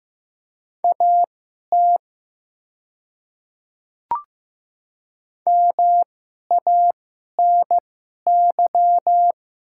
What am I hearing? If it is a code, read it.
Morse code, heard as a keyed tone, about 15 wpm, spaced as for 11 wpm, 690 Hz. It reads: AT MANY
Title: 2x Character Spacing - Top 100 Words - 15wpm